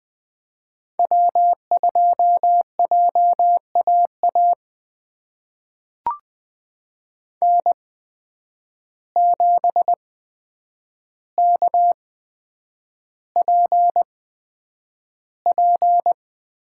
Morse code, heard as a keyed tone, about 20 wpm, spaced as for 9 wpm, 700 Hz. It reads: W2JAA N7KPP
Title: USA Call Signs - 8x Character Spacing - 20wpm